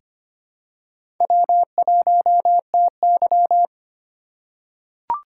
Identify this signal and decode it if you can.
Morse code, heard as a keyed tone, about 25 wpm, 700 Hz. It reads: W1TY
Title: USA Call Signs - 3x Character Spacing - 25wpm